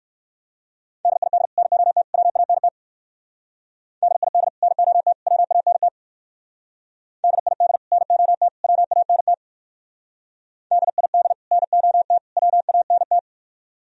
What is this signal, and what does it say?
Morse code, heard as a keyed tone, about 57 wpm, 700 Hz. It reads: DID NOT WANT DID NOT WANT DID NOT WANT DID NOT WANT